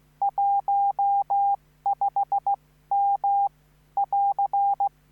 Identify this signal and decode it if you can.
Morse code, heard as a keyed tone, about 15 wpm, spaced as for 13 wpm, 800 Hz. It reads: 15M <AR>